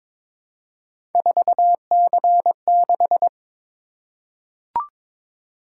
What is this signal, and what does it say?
Morse code, heard as a keyed tone, about 22 wpm, 700 Hz. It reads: 4C6
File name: practice / 22wpm / Three Letter-Number Combinations - 22wpm